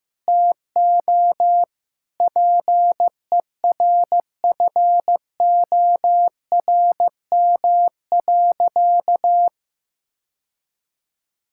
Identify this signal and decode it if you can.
Morse code, heard as a keyed tone, about 15 wpm, 700 Hz. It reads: TO PERFORM.
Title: Sentences from Top 1000 Words - Review Part 6 - 15wpm